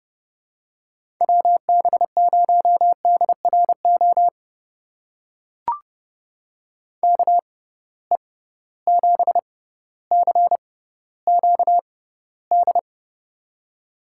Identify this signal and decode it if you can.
Morse code, heard as a keyed tone, about 30 wpm, spaced as for 15 wpm, 700 Hz. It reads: WB0DRO KE7CQD
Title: USA Call Signs - 6 Characters - 6x Character Spacing - 30wpm